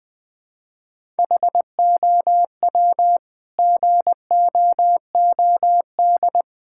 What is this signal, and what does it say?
Morse code, heard as a keyed tone, about 20 wpm, 700 Hz. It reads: HOW GOOD
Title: Sentences from Top 1000 Words - Review Part 6 - 20wpm